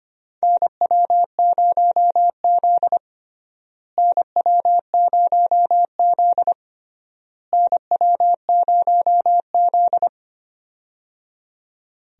Morse code, heard as a keyed tone, about 25 wpm, 700 Hz. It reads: NW0Z NW0Z NW0Z
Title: USA Call Signs - 4 Characters - Repeated 3x - 25wpm